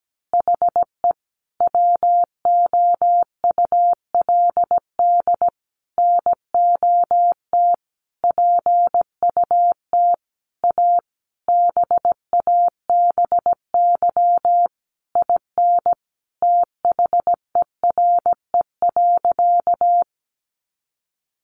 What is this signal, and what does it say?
Morse code, heard as a keyed tone, about 17 wpm, 700 Hz. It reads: HE WOULD NOT PUT A BABY IN THERE.
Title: Easy Phrases 17wpm